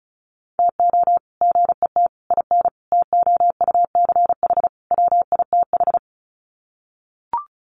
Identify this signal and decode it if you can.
Morse code, heard as a keyed tone, about 34 wpm, 700 Hz. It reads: TO GET IN TOUCH WITH
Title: Sets of 5 Words Encore2 35wpm